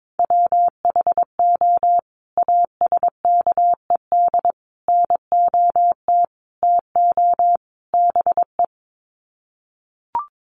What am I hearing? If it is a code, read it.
Morse code, heard as a keyed tone, about 22 wpm, 700 Hz. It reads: WHO ASKED NOT TO BE
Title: Sets of 5 Words Encore2 22wpm